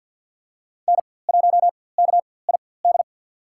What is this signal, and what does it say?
Morse code, heard as a keyed tone, about 48 wpm, spaced as for 23 wpm, 700 Hz. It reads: N1KID
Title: USA Call Signs - 5 Characters - 4x Character Spacing - 50wpm